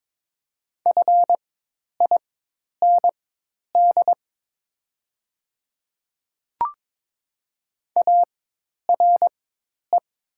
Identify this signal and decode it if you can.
Morse code, heard as a keyed tone, about 23 wpm, spaced as for 10 wpm, 700 Hz. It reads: FIND ARE
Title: 4x Character Spacing - Top 200 Words - 22wpm